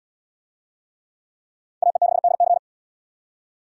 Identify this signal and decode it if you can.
Morse code, heard as a keyed tone, about 74 wpm, 700 Hz. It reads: AE6DC